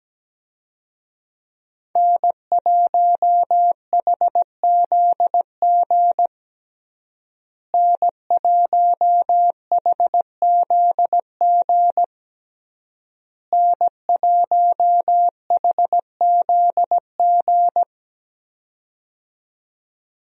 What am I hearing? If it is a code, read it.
Morse code, heard as a keyed tone, about 17 wpm, 700 Hz. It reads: N1HZG N1HZG N1HZG
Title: USA Call Signs - 5 Characters - Repeated 3x - 17wpm